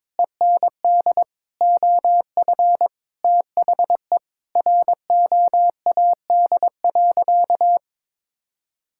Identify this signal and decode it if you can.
Morse code, heard as a keyed tone, about 22 wpm, 700 Hz. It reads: END OF THE ROAD.
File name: practice / 22wpm / Sentences with 4-Word Phrases - 22wpm